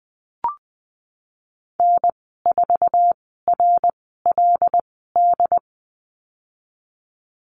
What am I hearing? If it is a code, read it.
Morse code, heard as a keyed tone, about 20 wpm, spaced as for 15 wpm, 700 Hz. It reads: N4RLD
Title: 154 Lesson 25g - Letter p - Call Signs 20wpm